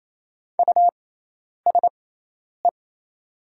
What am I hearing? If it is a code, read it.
Morse code, heard as a keyed tone, about 28 wpm, spaced as for 10 wpm, 700 Hz. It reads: USE
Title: 6x Character Spacing - Top 100 Words - 28wpm